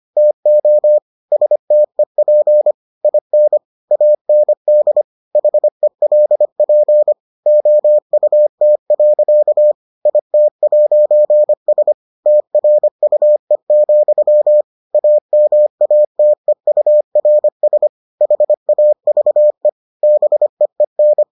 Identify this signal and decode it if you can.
Morse code, heard as a keyed tone, about 25 wpm, 600 Hz. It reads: TO STEP IN AND HELP OUT. IT'S TRUE, AMATEURS HAVE BEEN